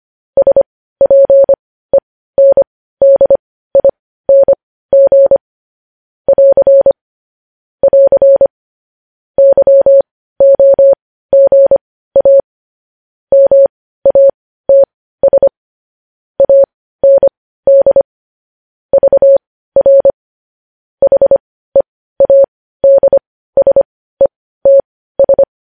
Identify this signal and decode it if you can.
Morse code, heard as a keyed tone, about 25 wpm, spaced as for 15 wpm, 550 Hz. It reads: SPENDING <AR> <AR> YOGA MATS AND VR HEADSETS
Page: news headlines in morse code at 15 wpm